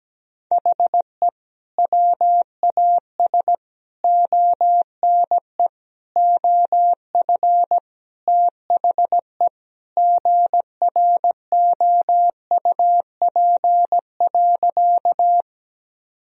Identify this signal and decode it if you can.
Morse code, heard as a keyed tone, about 17 wpm, 700 Hz. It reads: HE WAS ONE OF THE GROUP.
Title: Sentences from Top 1000 Words - Review Part 1 - 17wpm